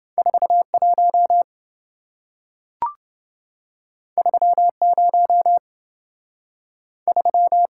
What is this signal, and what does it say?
Morse code, heard as a keyed tone, about 27 wpm, 700 Hz. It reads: 41 30 3